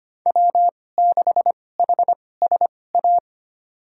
Morse code, heard as a keyed tone, about 25 wpm, spaced as for 18 wpm, 700 Hz. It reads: W6HSA
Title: USA Call Signs - 5 Characters - 2x Character Spacing - 25wpm